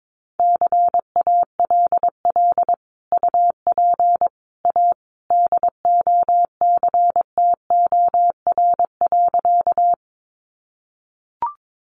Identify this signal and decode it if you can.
Morse code, heard as a keyed tone, about 22 wpm, 700 Hz. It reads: CALL UP A DOCTOR.
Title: Sentences from Top 1500 Words 22wpm